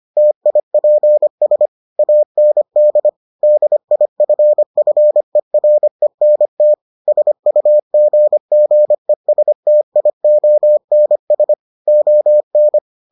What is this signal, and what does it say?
Morse code, heard as a keyed tone, about 25 wpm, 600 Hz. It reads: TIPS AND DIFFERENT SUGGESTIONS ON